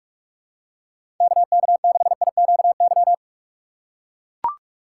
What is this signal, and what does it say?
Morse code, heard as a keyed tone, about 44 wpm, 700 Hz. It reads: KK6IQY
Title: USA Call Signs - 6 Characters - 6x Character Spacing - 45wpm